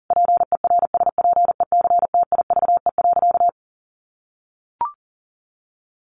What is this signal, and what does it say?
Morse code, heard as a keyed tone, about 40 wpm, 700 Hz. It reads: PERSPECTIVE.
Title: Sentences from Top 1200 Words 40wpm